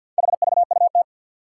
Morse code, heard as a keyed tone, about 46 wpm, 700 Hz. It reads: HVUT
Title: Four Challenging Letter-Number Combinations 50wpm